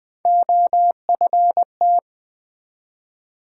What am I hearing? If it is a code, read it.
Morse code, heard as a keyed tone, about 20 wpm, 700 Hz. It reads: OFT
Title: Build A Word - 1 Letter At A Time - 5-Letter Words - 20wpm